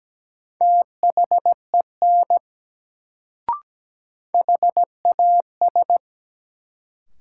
Morse code, heard as a keyed tone, about 17 wpm, 700 Hz. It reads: THEN HAS
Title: Sentences from Top 100 Words - New Word Drill - 17wpm